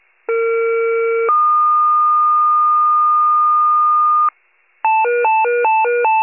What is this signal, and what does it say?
Two-tone page, heard as a one-second tone, then a three-second tone, about 460 then 1220 Hz. It is heard from 0.3 to 4.3 s.
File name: wcalert